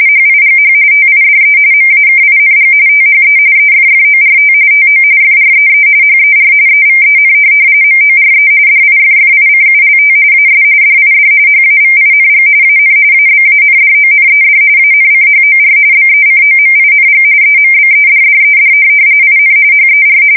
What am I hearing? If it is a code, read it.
RTTY, about 45.45 baud. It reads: HIGH TIDE MAKING THE EFFORT TO FREE HER EVEN MORE DIFFICULT AS DID HAVING AN ABANDONED ANCHOR BECOMING EMBEDDED IN HER
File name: fsk170 / USS-Missouri